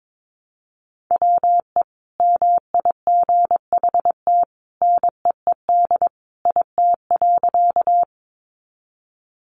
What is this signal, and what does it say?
Morse code, heard as a keyed tone, about 22 wpm, 700 Hz. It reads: WE MIGHT NEED IT.
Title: A Sound Of Thunder 22wpm